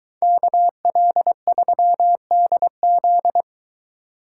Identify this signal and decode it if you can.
Morse code, heard as a keyed tone, about 22 wpm, 700 Hz. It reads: KL3DZ